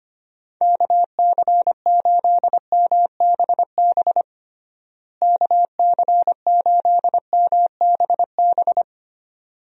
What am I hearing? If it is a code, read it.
Morse code, heard as a keyed tone, about 25 wpm, 700 Hz. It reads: KC8MBB KC8MBB
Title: USA Call Signs - 6 Characters - Repeated 2x - 25wpm